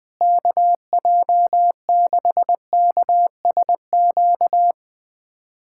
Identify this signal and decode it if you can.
Morse code, heard as a keyed tone, about 20 wpm, 700 Hz. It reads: KJ6KSQ
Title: USA Callsign DE Callsign 20wpm